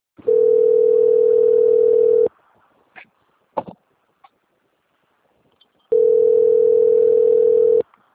Ringback tone (North America), 440 + 480 Hz.